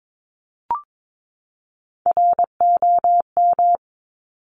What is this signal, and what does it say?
Morse code, heard as a keyed tone, about 22 wpm, 700 Hz. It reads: ROM
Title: Top Three Letter Combinations in Words 22wpm